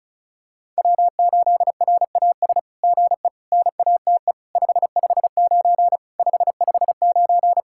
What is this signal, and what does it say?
Morse code, heard as a keyed tone, about 36 wpm, 700 Hz. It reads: W8RAS GE NATE 559 559
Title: SOTA QSO Practice 35wpm